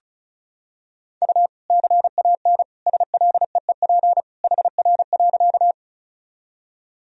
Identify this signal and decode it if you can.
Morse code, heard as a keyed tone, about 36 wpm, 700 Hz. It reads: U CAN SLEEP HR.
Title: Sentences from Top 500 Words and Abbreviations 35wpm